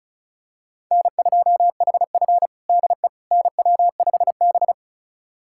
Morse code, heard as a keyed tone, about 34 wpm, 700 Hz. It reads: N2HF DE NW5B
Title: USA Callsign DE Callsign - 4 Characters - 35wpm